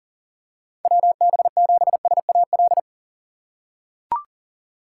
Morse code, heard as a keyed tone, about 37 wpm, 700 Hz. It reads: WB7SAL
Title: USA Call Signs - 6 Characters - 2x Character Spacing - 40wpm